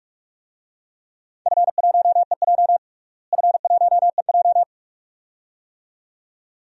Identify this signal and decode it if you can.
Morse code, heard as a keyed tone, about 46 wpm, 700 Hz. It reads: F1EJ F1EJ
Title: DXCC-100 Call Signs - 4 Characters - Repeated 2x - 45wpm